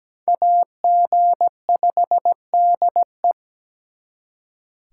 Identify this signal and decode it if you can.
Morse code, heard as a keyed tone, about 20 wpm, 700 Hz. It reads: AG5DE
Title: Speed Racing - USA 2x2 Call Signs 17wpm